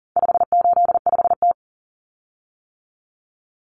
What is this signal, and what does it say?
Morse code, heard as a keyed tone, about 40 wpm, 700 Hz. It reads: 585T
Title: Common Cut Numbers - 4-Digit 40wpm